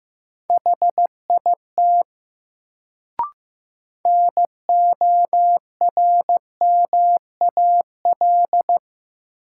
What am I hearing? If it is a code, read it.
Morse code, heard as a keyed tone, about 15 wpm, 700 Hz. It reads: HIT NORMAL
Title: Sentences from Top 1200 Words - New Word Drill - 15wpm